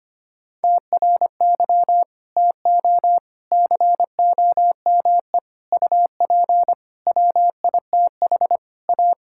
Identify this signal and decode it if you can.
Morse code, heard as a keyed tone, about 25 wpm, 700 Hz. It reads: TRY TO COME UP WITH A
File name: practice / 25wpm / Sentences with 4-Word Phrases - 25wpm